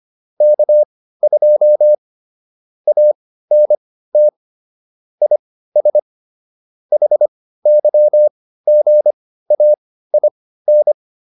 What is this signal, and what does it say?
Morse code, heard as a keyed tone, about 25 wpm, spaced as for 15 wpm, 600 Hz. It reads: K2 ANT IS HYGAIN